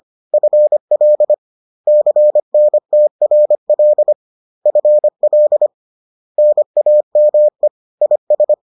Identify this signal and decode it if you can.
Morse code, heard as a keyed tone, about 25 wpm, 600 Hz. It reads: FL CNTRL FL NAME IS